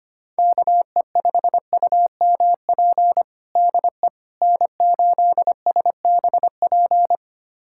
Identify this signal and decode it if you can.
Morse code, heard as a keyed tone, about 25 wpm, 700 Hz. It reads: KE5UMP DE N8SBP
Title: USA Callsign DE Callsign 25wpm